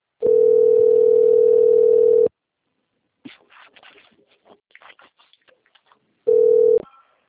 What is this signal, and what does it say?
Ringback tone (North America), 440 + 480 Hz.